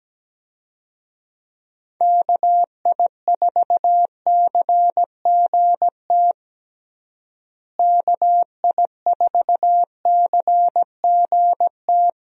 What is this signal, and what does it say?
Morse code, heard as a keyed tone, about 17 wpm, 700 Hz. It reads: KI4CGT KI4CGT
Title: USA Call Signs - 6 Characters - Repeated 2x - 17wpm